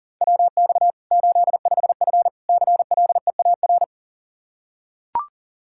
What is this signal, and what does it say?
Morse code, heard as a keyed tone, about 40 wpm, 700 Hz. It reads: WX 85F CLEAR
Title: Practice QSO - Exchange 2 - Info - 40wpm